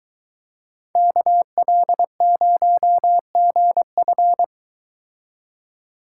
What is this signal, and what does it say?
Morse code, heard as a keyed tone, about 23 wpm, 700 Hz. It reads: KL0GF